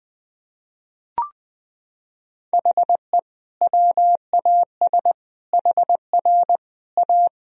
Morse code, heard as a keyed tone, about 20 wpm, 700 Hz. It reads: HE WAS HR A
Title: Sentences from Top 500 Words and Abbreviations 20wpm